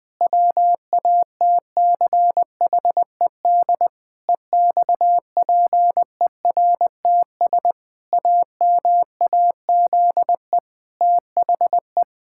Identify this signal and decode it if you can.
Morse code, heard as a keyed tone, about 20 wpm, 700 Hz. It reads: WATCHED EXPERTS AMAZE THE
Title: Pangram Challenge v2 - 20wpm